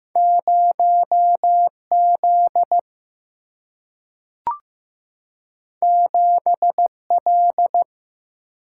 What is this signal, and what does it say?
Morse code, heard as a keyed tone, about 15 wpm, 700 Hz. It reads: 0 Z 7 L
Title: Single-Letter Single-Number 15wpm